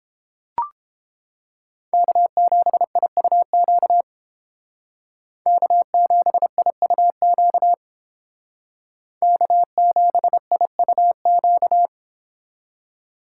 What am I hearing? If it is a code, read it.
Morse code, heard as a keyed tone, about 31 wpm, 700 Hz. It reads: K7IUQ K7IUQ K7IUQ